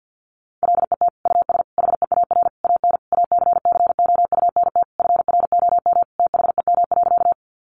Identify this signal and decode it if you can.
Morse code, heard as a keyed tone, about 50 wpm, 700 Hz. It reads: LET US HEAR AN ACCOUNT FROM THEM.
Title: Sentences from Top 900 Words 50wpm